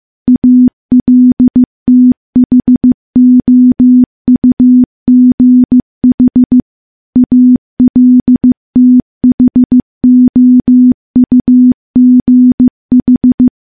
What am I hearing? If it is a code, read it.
Morse code, heard as a keyed tone, about 15 wpm, 250 Hz. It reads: ALTHOUGH ALTHOUGH
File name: morse / morse-training-15WPM